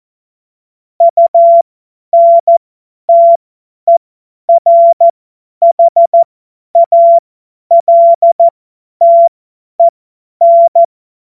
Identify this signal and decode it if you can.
Morse code, heard as a keyed tone, about 14 wpm, spaced as for 10 wpm, 680 Hz. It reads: UNTERHALTEN
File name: Witz-08-Juli-2025-50-F70